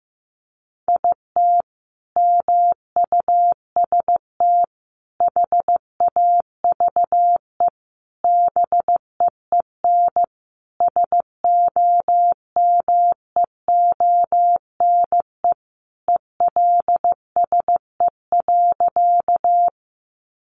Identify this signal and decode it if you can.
Morse code, heard as a keyed tone, about 15 wpm, 700 Hz. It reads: IT MUST HAVE BEEN SOMEONE ELSE.
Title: Sentences from Top 800 Words 15wpm